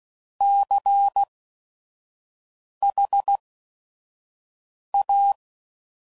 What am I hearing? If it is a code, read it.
Morse code, heard as a keyed tone, about 16 wpm, spaced as for 5 wpm, 800 Hz. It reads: CHA
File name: CW Code Groups @5WPM / Technical Subjects / XC